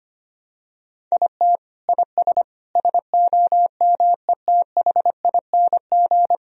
Morse code, heard as a keyed tone, about 25 wpm, 700 Hz. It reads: IT IS SOMETHING
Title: Sentences from Top 1000 Words - Review Part 3 - 25wpm